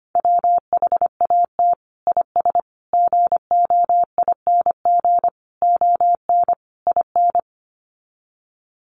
Morse code, heard as a keyed tone, about 25 wpm, 700 Hz. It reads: WHAT IS GOING ON IN
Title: Sets of 5 Words Encore2 25wpm